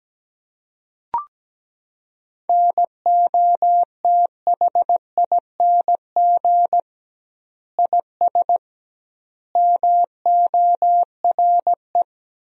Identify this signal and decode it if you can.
Morse code, heard as a keyed tone, about 17 wpm, 700 Hz. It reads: NOTHING IS MORE
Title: 2x Word-Spacing - Sentences from Top 500 Words - 17wpm